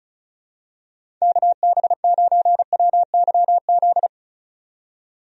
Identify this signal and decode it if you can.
Morse code, heard as a keyed tone, about 36 wpm, 700 Hz. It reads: KB9WYZ
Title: USA Call Signs - 6 Characters - 35wpm